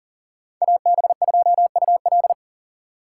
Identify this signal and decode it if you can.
Morse code, heard as a keyed tone, about 52 wpm, 700 Hz. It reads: AB2UL